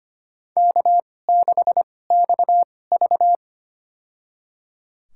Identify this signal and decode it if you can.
Morse code, heard as a keyed tone, about 25 wpm, spaced as for 18 wpm, 700 Hz. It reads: K6XV
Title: USA Call Signs - 4 Characters - 2x Character Spacing - 25wpm